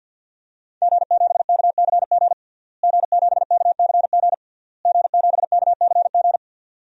Morse code, heard as a keyed tone, about 47 wpm, 700 Hz. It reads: G7KCG G7KCG G7KCG
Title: DXCC-100 Call Signs - 5 Characters - Repeated 3x - 50wpm